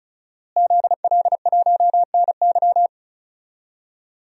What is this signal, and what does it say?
Morse code, heard as a keyed tone, about 36 wpm, 700 Hz. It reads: ZL1NY
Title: DXCC-100 Call Signs - 5 Characters - Repeated 3x - 35wpm